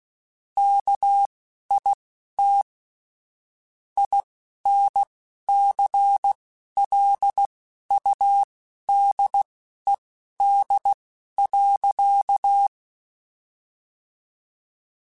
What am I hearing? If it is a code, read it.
Morse code, heard as a keyed tone, about 16 wpm, spaced as for 11 wpm, 800 Hz. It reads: KIT INCLUDED.
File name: Chuckles / Chuckles-mp3s / Guaranteed @10